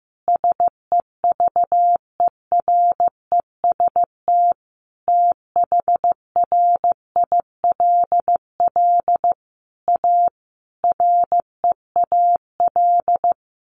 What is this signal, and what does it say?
Morse code, heard as a keyed tone, about 15 wpm, 700 Hz. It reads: SEVEREST THRILL A REAL
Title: A Sound Of Thunder 15wpm